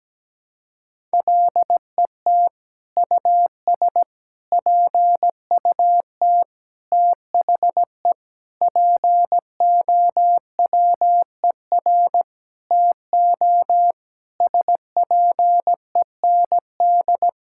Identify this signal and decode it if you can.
Morse code, heard as a keyed tone, about 17 wpm, 700 Hz. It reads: LET US PUT THE POWER TO SPEND
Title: Sentences from Top 1000 Words - Review Part 5 - 17wpm